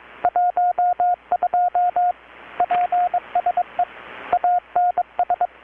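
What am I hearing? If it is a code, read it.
Morse code, heard as a keyed tone, about 20 wpm, 720 Hz, with radio noise. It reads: 12 PSE ANS
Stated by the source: Morse from KPH's Night of Nights XX maritime radiotelegraphy event, received in Chiapas, Mexico about 3,800 km away on a Kenwood R-600 with a grounded 10 m long-wire antenna.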